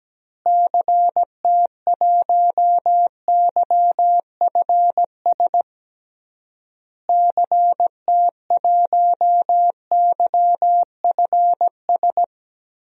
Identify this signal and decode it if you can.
Morse code, heard as a keyed tone, about 17 wpm, 700 Hz. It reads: CT1YFS CT1YFS
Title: DXCC-100 Call Signs - 6 Characters - Repeated 3x - 17wpm